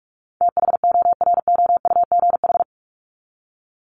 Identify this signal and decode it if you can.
Morse code, heard as a keyed tone, about 44 wpm, 700 Hz. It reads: THOROUGH
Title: LY Suffix - Top 100 Words with Base Words 45wpm